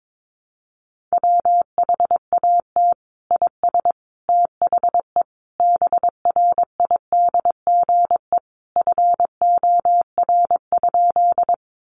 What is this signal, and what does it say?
Morse code, heard as a keyed tone, about 22 wpm, 700 Hz. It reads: WHAT IS THE BRIDGE FOR?
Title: Sentences from Top 1200 Words 22wpm